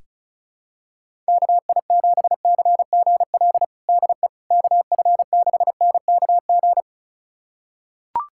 Morse code, heard as a keyed tone, about 36 wpm, 700 Hz. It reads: KI7CGL DE KF6NKG
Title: USA Callsign DE Callsign - 6 Characters - 35wpm